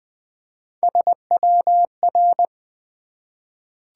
Morse code, heard as a keyed tone, about 20 wpm, 700 Hz. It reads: SWR
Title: Top 180 QSO Elements 20wpm